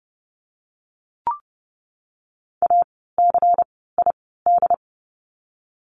Morse code, heard as a keyed tone, about 30 wpm, 700 Hz. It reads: A C I D
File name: Lesson 16f - Letter c - Words 30wpm